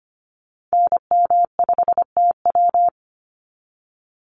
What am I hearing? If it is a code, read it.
Morse code, heard as a keyed tone, about 25 wpm, 690 Hz. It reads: NM5TW
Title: NA SOTA Activator Callsigns 25wpm